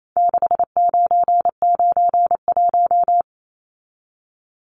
Morse code, heard as a keyed tone, about 28 wpm, 700 Hz. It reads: 6991
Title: Numbers - 4-Digit 28wpm